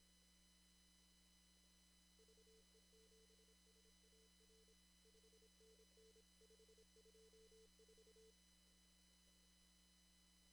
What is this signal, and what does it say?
Morse code, heard as a keyed tone, about 26 wpm, 440 Hz. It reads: VE7SAR 5NN524
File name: CW / kt5j wpx cw-op-k5pi